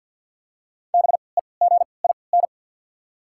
Morse code, heard as a keyed tone, about 54 wpm, spaced as for 29 wpm, 700 Hz. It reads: BEGIN
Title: 3x Character Spacing - Top 500 Words - 50wpm